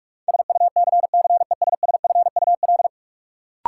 Morse code, heard as a keyed tone, about 46 wpm, 700 Hz. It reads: SUCCESSFUL
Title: Sentences from Top 900 Words - New Word Drill - 45wpm